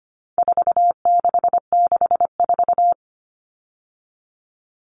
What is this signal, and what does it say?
Morse code, heard as a keyed tone, about 25 wpm, 700 Hz. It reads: 4664
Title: All Cut Numbers - 4-Digit 25wpm